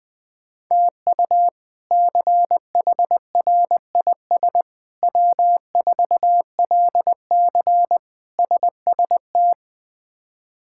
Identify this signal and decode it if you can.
Morse code, heard as a keyed tone, about 20 wpm, 700 Hz. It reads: TU CHRIS W4LC SST
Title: K1USN SST QSO Practice - 20wpm